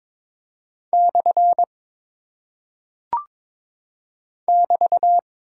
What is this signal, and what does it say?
Morse code, heard as a keyed tone, about 22 wpm, 700 Hz. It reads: / <BT>
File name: practice / 22wpm / Single Punctuation-Prosign - 22wpm